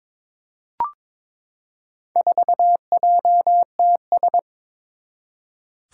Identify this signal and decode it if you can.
Morse code, heard as a keyed tone, about 22 wpm, 700 Hz. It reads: 4JTS
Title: Four Challenging Letter-Number Combinations 22wpm